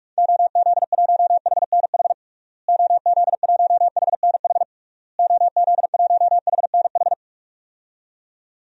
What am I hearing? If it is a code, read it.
Morse code, heard as a keyed tone, about 42 wpm, 700 Hz. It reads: OZ1HNH OZ1HNH OZ1HNH
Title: DXCC-100 Call Signs - 6 Characters - Repeated 3x - 45wpm